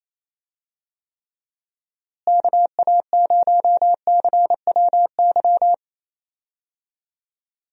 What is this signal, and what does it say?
Morse code, heard as a keyed tone, about 28 wpm, 700 Hz. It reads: KA0CWY
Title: USA Call Signs - 6 Characters - Repeated 3x - 28wpm